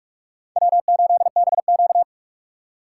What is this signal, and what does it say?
Morse code, heard as a keyed tone, about 46 wpm, 700 Hz. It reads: W8BQ
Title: USA Call Signs - 4 Characters - 4x Character Spacing - 45wpm